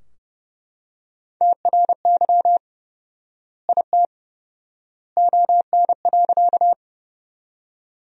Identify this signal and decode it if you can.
Morse code, heard as a keyed tone, about 30 wpm, 700 Hz. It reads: TRY IT ON.
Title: 4x Word-Spacing - Sentences from Top 200 Words - 30wpm